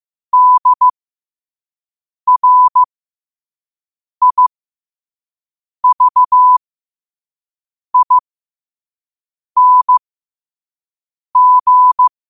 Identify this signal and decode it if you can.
Morse code, heard as a keyed tone, about 15 wpm, spaced as for 5 wpm, 1000 Hz. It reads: DRIVING
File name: qotdfiles / 5wpm